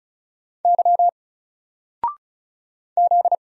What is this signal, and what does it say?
Morse code, heard as a keyed tone, about 35 wpm, 700 Hz. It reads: Y Z
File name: Single-Letter-Number-35wpm